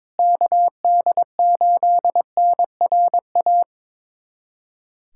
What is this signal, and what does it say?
Morse code, heard as a keyed tone, about 22 wpm, 700 Hz. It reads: KD8NRA
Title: USA Callsign DE Callsign 22wpm